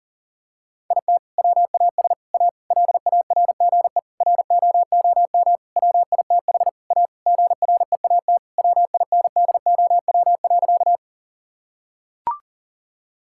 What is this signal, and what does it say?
Morse code, heard as a keyed tone, about 40 wpm, 700 Hz. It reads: IT WAS A LARGE ROOM WITH A GREAT WINDOW.
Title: Sentences from Top 1600 Words 40wpm